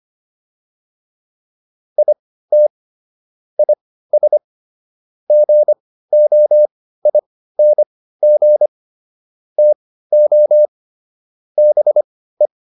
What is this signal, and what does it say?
Morse code, heard as a keyed tone, about 25 wpm, spaced as for 15 wpm, 600 Hz. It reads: IT IS GOING TO BE